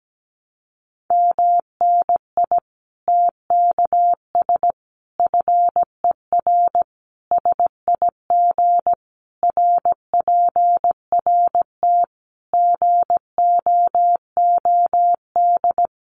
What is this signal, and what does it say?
Morse code, heard as a keyed tone, about 17 wpm, 700 Hz. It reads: MNI TKS FER SIG RPRT GOOD
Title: Common QSO Phrases 17wpm